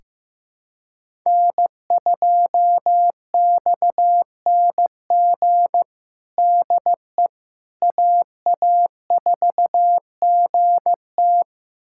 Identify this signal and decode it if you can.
Morse code, heard as a keyed tone, about 15 wpm, 700 Hz. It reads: N2XNG DE AA4GT
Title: USA Callsign DE Callsign - 5 Characters - 15wpm